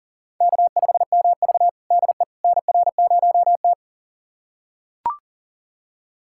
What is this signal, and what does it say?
Morse code, heard as a keyed tone, about 40 wpm, 700 Hz. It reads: K5MV DE NR0T
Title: USA Callsign DE Callsign - 4 Characters - 40wpm